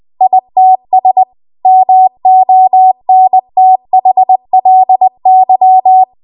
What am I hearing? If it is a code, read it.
Morse code, heard as a keyed tone, about 20 wpm, 750 Hz. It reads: ITS MONTHLY
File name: Morse / 20WPM